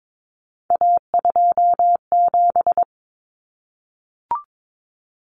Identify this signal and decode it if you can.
Morse code, heard as a keyed tone, about 22 wpm, 700 Hz. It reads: A27
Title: Common Cut Numbers - 3-Digit 22wpm